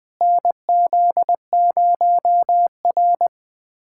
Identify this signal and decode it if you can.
Morse code, heard as a keyed tone, about 20 wpm, 700 Hz. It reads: NZ0R
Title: USA Callsign DE Callsign - 4 Characters - 20wpm